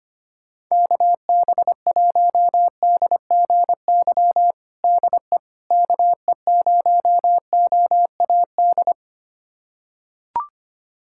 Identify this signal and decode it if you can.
Morse code, heard as a keyed tone, about 25 wpm, 700 Hz. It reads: KB1DGY DE KE0OAD
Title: USA Callsign DE Callsign - 6 Characters - 25wpm